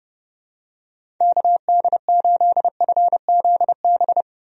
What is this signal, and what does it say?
Morse code, heard as a keyed tone, about 30 wpm, 700 Hz. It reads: KD8FZB
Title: USA Call Signs - 6 Characters - 30wpm